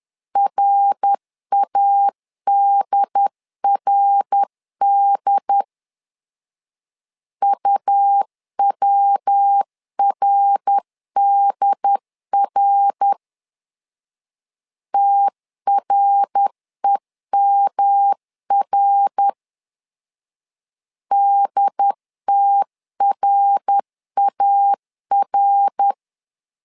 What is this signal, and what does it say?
Morse code, heard as a keyed tone, about 10 wpm, 800 Hz. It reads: RADRD UWRDR TREMR DTRAR
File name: LEZIONI / 3 MWRO / 3 MWR 10wpm 1952